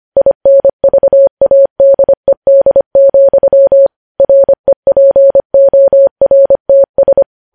Morse code, heard as a keyed tone, about 25 wpm, 550 Hz. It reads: INVADED, REPORTS
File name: podcast / News headlines-1x-25x25